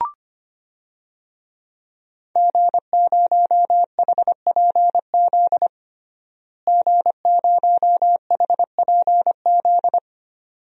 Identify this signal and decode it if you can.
Morse code, heard as a keyed tone, about 25 wpm, 700 Hz. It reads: G0HPZ G0HPZ
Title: DXCC-100 Call Signs - 5 Characters - Repeated 2x - 25wpm